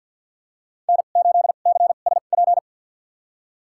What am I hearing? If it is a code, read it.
Morse code, heard as a keyed tone, about 51 wpm, spaced as for 36 wpm, 700 Hz. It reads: N8CSP
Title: USA Call Signs - 5 Characters - 2x Character Spacing - 50wpm